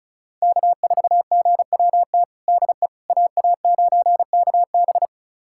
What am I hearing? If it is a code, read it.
Morse code, heard as a keyed tone, about 34 wpm, 700 Hz. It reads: K4GWT DE AA9KB
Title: USA Callsign DE Callsign 35wpm